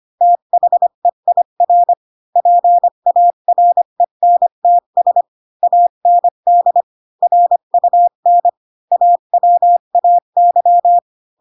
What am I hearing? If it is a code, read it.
Morse code, heard as a keyed tone, about 25 wpm, 700 Hz. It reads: THEIR PARENTS AND RUN AWAY
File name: Pinocchio / Pinocchio ch4 20 25 ik1hge